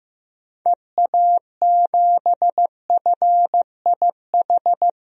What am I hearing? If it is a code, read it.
Morse code, heard as a keyed tone, about 15 wpm, 700 Hz. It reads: EA7FIH
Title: DXCC-100 Call Signs - 6 Characters - 15wpm